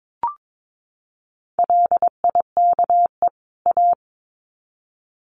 Morse code, heard as a keyed tone, about 22 wpm, 700 Hz. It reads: LIKE A
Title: Sets of 2 Words 22wpm